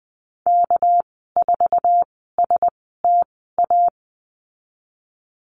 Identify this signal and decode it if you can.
Morse code, heard as a keyed tone, about 20 wpm, spaced as for 15 wpm, 700 Hz. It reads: K4STA
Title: 207 Lesson 33g - Letter j - Call Signs 20wpm